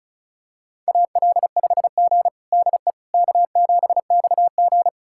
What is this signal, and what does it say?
Morse code, heard as a keyed tone, about 34 wpm, 700 Hz. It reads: AL5G DE K7XG
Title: USA Callsign DE Callsign - 4 Characters - 35wpm